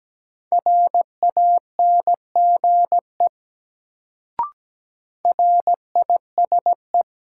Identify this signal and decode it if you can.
Morse code, heard as a keyed tone, about 17 wpm, 700 Hz. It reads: RANGE RISE
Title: Sentences from Top 900 Words - New Word Drill - 17wpm